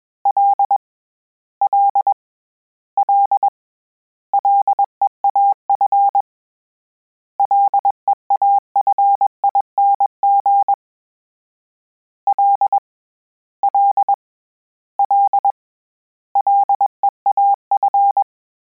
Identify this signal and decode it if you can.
Morse code, heard as a keyed tone, about 21 wpm, 800 Hz. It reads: L L L LEAF LEAFING L L L LEAF
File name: Combinations / Combinations-mp3s / Combinations@20